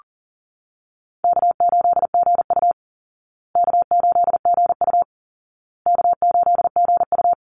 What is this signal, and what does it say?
Morse code, heard as a keyed tone, about 40 wpm, 700 Hz. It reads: K8GU K8GU K8GU
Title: Easy Call Signs Repeated 3x - 40wpm